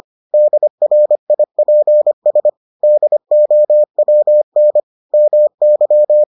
Morse code, heard as a keyed tone, about 25 wpm, 600 Hz. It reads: DRIPS DOWN MY